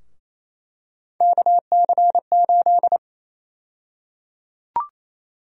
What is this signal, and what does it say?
Morse code, heard as a keyed tone, about 27 wpm, 700 Hz. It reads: KC8
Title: USA Call Sign Prefixes - 28wpm